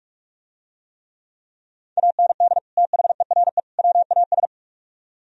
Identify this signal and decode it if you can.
Morse code, heard as a keyed tone, about 44 wpm, 700 Hz. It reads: AND THERE WAS